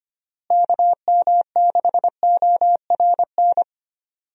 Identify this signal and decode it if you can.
Morse code, heard as a keyed tone, about 25 wpm, 700 Hz. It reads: KM6ORN
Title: USA Call Signs - 6 Characters - Repeated 2x - 25wpm